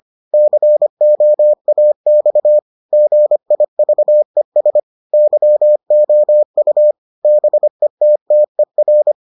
Morse code, heard as a keyed tone, about 25 wpm, 600 Hz. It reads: COAX GIVES YOU BETTER